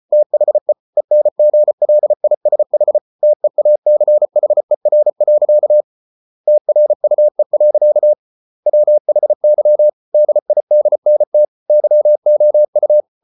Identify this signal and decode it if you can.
Morse code, heard as a keyed tone, about 34 wpm, 600 Hz. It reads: THE ENGLISH TEACHER. TRUE. WHY DIDNT YOU